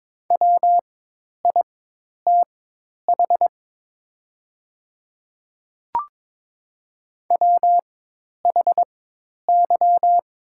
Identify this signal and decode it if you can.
Morse code, heard as a keyed tone, about 22 wpm, spaced as for 10 wpm, 700 Hz. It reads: WITH WHY
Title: 4x Character Spacing - Top 200 Words - 22wpm